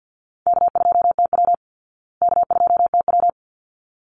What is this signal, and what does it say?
Morse code, heard as a keyed tone, about 49 wpm, 700 Hz. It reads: K2TW K2TW
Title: Easy Call Signs Repeated 2x - 50wpm